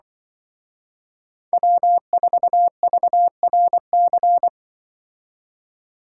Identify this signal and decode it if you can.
Morse code, heard as a keyed tone, about 24 wpm, 700 Hz. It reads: W4VRC
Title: Speed Racing - USA 1x3 Call Signs 20wpm